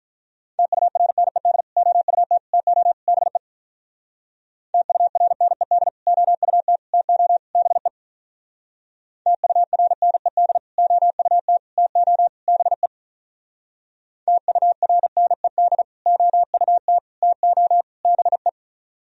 Morse code, heard as a keyed tone, about 44 wpm, 700 Hz. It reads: TURNED OUT TO BE TURNED OUT TO BE TURNED OUT TO BE TURNED OUT TO BE